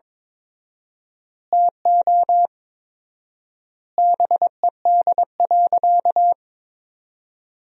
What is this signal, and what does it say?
Morse code, heard as a keyed tone, about 22 wpm, 700 Hz. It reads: TO BED.
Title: 4x Word-Spacing - Sentences from Top 500 Words - 22wpm